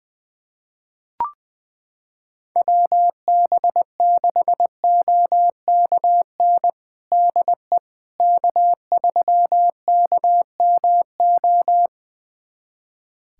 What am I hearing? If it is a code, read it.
Morse code, heard as a keyed tone, about 20 wpm, 700 Hz. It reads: WB6OKN DE K3KMO
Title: USA Callsign DE Callsign 20wpm